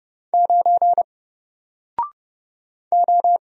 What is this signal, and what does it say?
Morse code, heard as a keyed tone, about 30 wpm, 700 Hz. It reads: 9 O
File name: Single-Letter-Number-30wpm